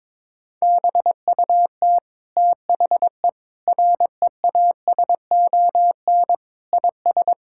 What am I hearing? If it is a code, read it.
Morse code, heard as a keyed tone, about 22 wpm, 700 Hz. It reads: BUT THE REASON IS
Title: Sentences from Top 1600 Words 22wpm